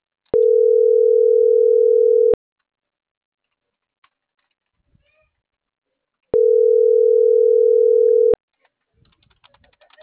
Ringback tone (North America), 440 + 480 Hz.